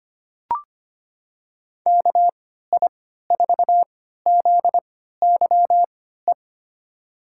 Morse code, heard as a keyed tone, about 25 wpm, spaced as for 14 wpm, 700 Hz. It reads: KI4ZYE
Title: USA Call Signs - 3x Character Spacing - 25wpm